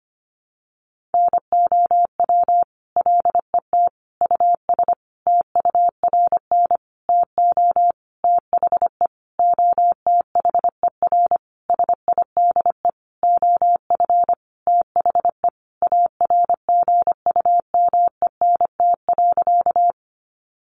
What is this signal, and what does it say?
Morse code, heard as a keyed tone, about 25 wpm, 700 Hz. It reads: NOW LET US TURN TO THE OTHER SIDE OF THE ARGUMENT.
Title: Sentences from Top 1400 Words 25wpm